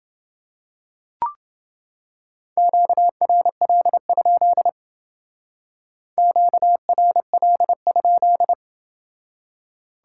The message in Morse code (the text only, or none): QRL? QRL?